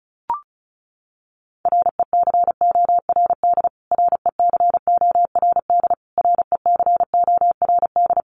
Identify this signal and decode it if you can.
Morse code, heard as a keyed tone, about 34 wpm, 700 Hz. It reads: RECORD RECORD RECORD
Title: Top 500 Words Repeated 35wpm